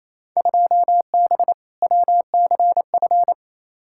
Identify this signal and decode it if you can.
Morse code, heard as a keyed tone, about 28 wpm, 700 Hz. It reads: 2B WCF
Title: Field Day Exchanges Encore - 28wpm